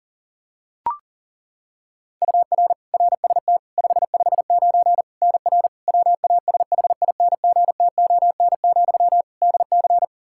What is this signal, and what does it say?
Morse code, heard as a keyed tone, about 37 wpm, 700 Hz. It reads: UR RST 559 NR WASHINGTON, DC
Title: K1USN SST QSO Practice 40wpm